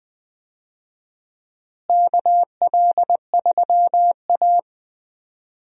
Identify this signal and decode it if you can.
Morse code, heard as a keyed tone, about 20 wpm, 700 Hz. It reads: KL3A
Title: USA Call Signs - 4 Characters - Repeated 2x - 20wpm